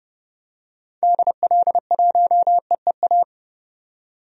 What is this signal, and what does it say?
Morse code, heard as a keyed tone, about 30 wpm, 700 Hz. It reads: DL1EEA
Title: DXCC-100 Call Signs - 6 Characters - 30wpm